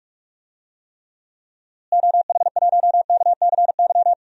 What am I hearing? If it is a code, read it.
Morse code, heard as a keyed tone, about 43 wpm, 700 Hz. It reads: OH1KCY
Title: DXCC-100 Call Signs - 6 Characters - Repeated 3x - 45wpm